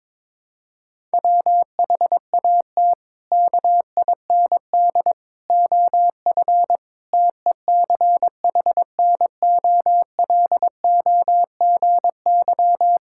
Morse code, heard as a keyed tone, about 22 wpm, 700 Hz. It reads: WHAT KIND OF TECHNOLOGY
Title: Sentences from Top 1000 Words - Review Part 5 - 22wpm